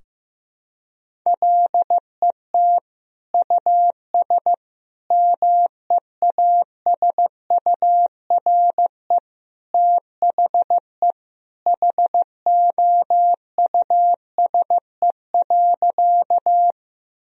Morse code, heard as a keyed tone, about 15 wpm, 700 Hz. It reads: LET US MEASURE THE HOUSE.
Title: Sentences from Top 1000 Words - Review Part 2 - 15wpm